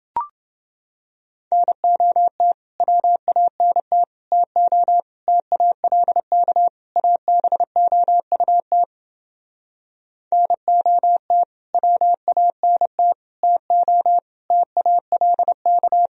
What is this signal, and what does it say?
Morse code, heard as a keyed tone, about 30 wpm, 700 Hz. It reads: NOT WANT TO TALK ABOUT NOT WANT TO TALK